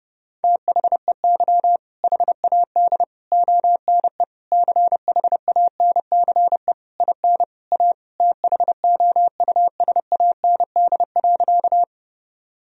Morse code, heard as a keyed tone, about 32 wpm, 700 Hz. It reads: THEY HAD ONE CHANCE IN A THOUSAND.
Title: Sentences from Top 1000 Words - Review Part 3 - 30wpm